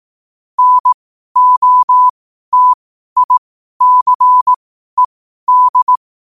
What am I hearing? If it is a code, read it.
Morse code, heard as a keyed tone, about 18 wpm, spaced as for 12 wpm, 1000 Hz. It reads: NOTICED
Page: Quotes for Wed, 13 Aug 2025 in Morse Code at 12 words per minute.